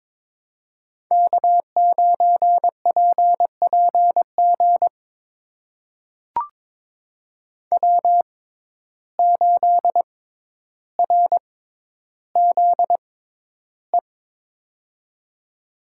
Morse code, heard as a keyed tone, about 22 wpm, spaced as for 11 wpm, 700 Hz. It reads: K9PPG W8RZE
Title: USA Call Signs - 5 Characters - 6x Character Spacing - 22wpm